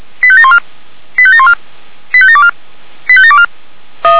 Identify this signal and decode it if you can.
Five-tone selective call (ZVEI), heard as quick runs of five tones, tones of about 70 ms. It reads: 85714 85714 85724 85724